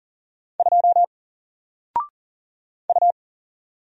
Morse code, heard as a keyed tone, about 40 wpm, 700 Hz. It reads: 2 U
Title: Single Letter-Number 40wpm